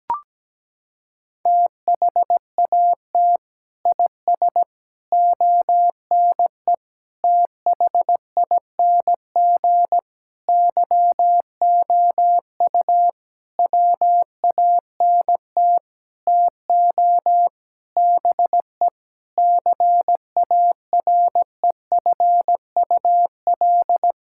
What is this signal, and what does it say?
Morse code, heard as a keyed tone, about 17 wpm, 700 Hz. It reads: THAT IS ONE THING YOU WANT TO BE CAREFUL
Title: Sentences from Top 1600 Words 17wpm